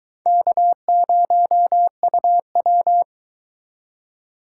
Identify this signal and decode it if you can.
Morse code, heard as a keyed tone, about 24 wpm, 700 Hz. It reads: K0UW